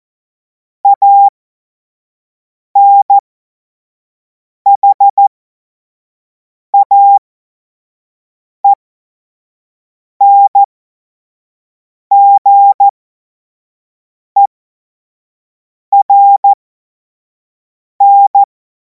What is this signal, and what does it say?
Morse code, heard as a keyed tone, about 14 wpm, spaced as for 5 wpm, 800 Hz. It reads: ANHAENGERN
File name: Morsekurs / Witze / Witz 09. Juni 2026 25 F70